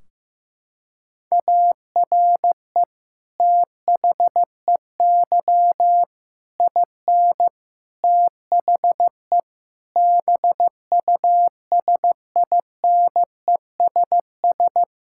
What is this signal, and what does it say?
Morse code, heard as a keyed tone, about 15 wpm, 700 Hz. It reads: ARE THEY IN THE BUSINESS
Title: Sentences from Top 1600 Words 15wpm